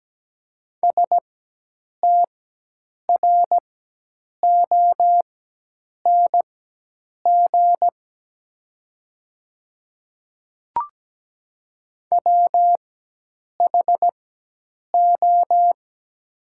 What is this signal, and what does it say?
Morse code, heard as a keyed tone, about 17 wpm, spaced as for 8 wpm, 700 Hz. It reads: STRONG WHO
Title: 4x Character Spacing - Top 500 Words - 17wpm